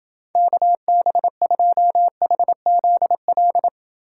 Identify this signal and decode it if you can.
Morse code, heard as a keyed tone, about 29 wpm, 700 Hz. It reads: KB2HZL